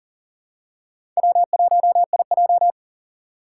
Morse code, heard as a keyed tone, about 38 wpm, 700 Hz. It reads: W1IJ
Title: USA Call Signs - 4 Characters - 3x Character Spacing - 40wpm